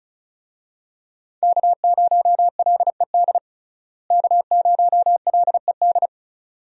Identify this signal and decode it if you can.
Morse code, heard as a keyed tone, about 36 wpm, 700 Hz. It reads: K0LED K0LED
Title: USA Call Signs - 5 Characters - Repeated 2x - 35wpm